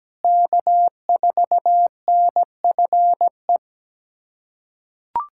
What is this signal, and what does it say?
Morse code, heard as a keyed tone, about 17 wpm, 700 Hz. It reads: K4NFE
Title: USA Callsign DE Callsign 17wpm